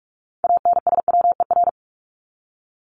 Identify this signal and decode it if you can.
Morse code, heard as a keyed tone, about 44 wpm, 700 Hz. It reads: ANSWER
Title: Top 500 Words 45wpm